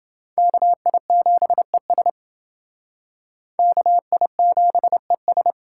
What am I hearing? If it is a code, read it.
Morse code, heard as a keyed tone, about 28 wpm, 700 Hz. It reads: KI7ES KI7ES